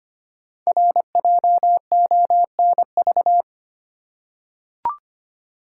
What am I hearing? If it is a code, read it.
Morse code, heard as a keyed tone, about 25 wpm, 700 Hz. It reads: RJONV
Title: Code Groups - 5 Letters - 25wpm